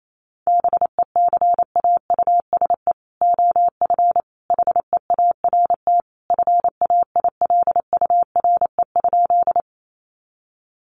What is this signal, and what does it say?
Morse code, heard as a keyed tone, about 28 wpm, 700 Hz. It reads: BECAUSE OF HEART FAILURE?
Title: Sentences from Top 1200 Words 28wpm